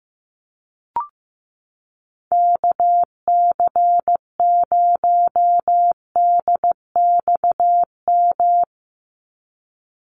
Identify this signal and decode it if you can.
Morse code, heard as a keyed tone, about 15 wpm, 700 Hz. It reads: KC0DXM
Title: USA Call Signs 15wpm